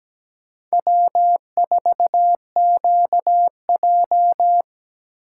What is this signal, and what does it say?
Morse code, heard as a keyed tone, about 17 wpm, 700 Hz. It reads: W4QJ
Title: USA Call Signs - 4 Characters - 4x Character Spacing - 17wpm